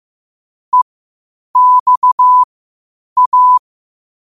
Morse code, heard as a keyed tone, about 15 wpm, spaced as for 8 wpm, 1000 Hz. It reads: EXA
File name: qotdfiles / 8wpm